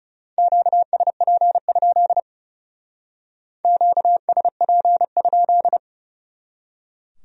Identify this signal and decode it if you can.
Morse code, heard as a keyed tone, about 39 wpm, 700 Hz. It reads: QSP? QSP?